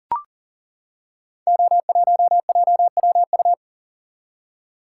Morse code, heard as a keyed tone, about 39 wpm, 690 Hz. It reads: O1JWU
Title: Code Groups - 5 Letter:Numbers - 40wpm